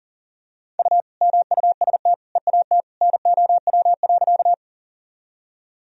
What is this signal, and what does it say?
Morse code, heard as a keyed tone, about 40 wpm, 700 Hz. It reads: U MUST EAT NOW.
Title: Sentences from Top 300 Words and Abbreviations 40wpm